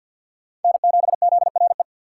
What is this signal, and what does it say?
Morse code, heard as a keyed tone, about 47 wpm, 700 Hz. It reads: N7ZRE
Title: USA Call Signs - 5 Characters - 6x Character Spacing - 50wpm